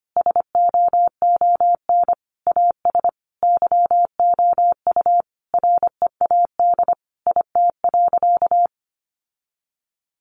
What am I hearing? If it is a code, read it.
Morse code, heard as a keyed tone, about 25 wpm, 700 Hz. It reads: SOON AS YOU READ IT.
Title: Sentences from Top 1300 Words 25wpm